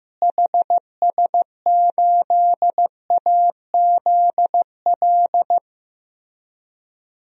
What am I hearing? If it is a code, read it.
Morse code, heard as a keyed tone, about 15 wpm, 700 Hz. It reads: HS8AZL
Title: DXCC-100 Call Signs - 6 Characters - Repeated 2x - 15wpm